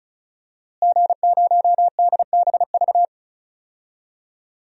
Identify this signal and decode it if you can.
Morse code, heard as a keyed tone, about 36 wpm, 700 Hz. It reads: G0DBV
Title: DXCC-100 Call Signs - 5 Characters - Repeated 2x - 35wpm